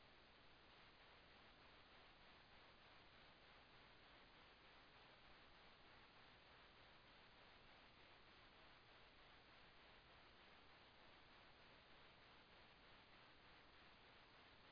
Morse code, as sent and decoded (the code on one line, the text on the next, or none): none